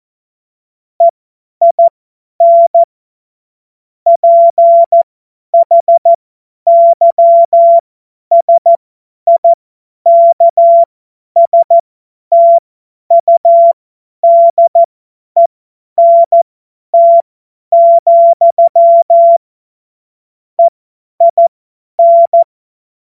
Morse code, heard as a keyed tone, about 14 wpm, spaced as for 10 wpm, 680 Hz. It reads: EIN PHYSIKSTUDENT, EIN
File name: Morsekurs / Witze / Witz 14. Juni 2022 50 F70